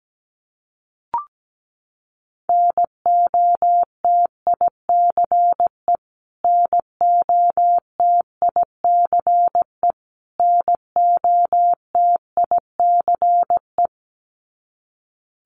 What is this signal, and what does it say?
Morse code, heard as a keyed tone, about 17 wpm, 700 Hz. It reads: NOTICE NOTICE NOTICE
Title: Top 500 Words Repeated 17wpm